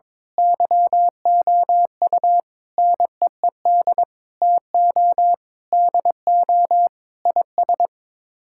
Morse code, heard as a keyed tone, about 22 wpm, 700 Hz. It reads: YOU NEED TO DO IS
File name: practice / 22wpm / Sets of 8 Words - 22wpm